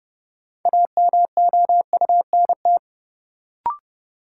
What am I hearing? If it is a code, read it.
Morse code, heard as a keyed tone, about 31 wpm, 700 Hz. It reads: AMOUNT
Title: Sentences from Top 700 Words - New Word Drill - 30wpm